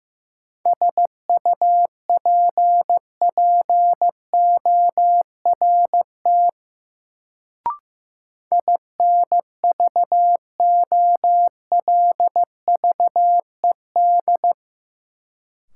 Morse code, heard as a keyed tone, about 16 wpm, 700 Hz. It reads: SUPPORT INVOLVED
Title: Sentences from Top 800 Words - New Word Drill - 15wpm